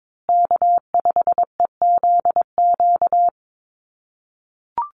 Morse code, heard as a keyed tone, about 22 wpm, 700 Hz. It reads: K5EZQ
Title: USA Call Signs Repeated 2x 22wpm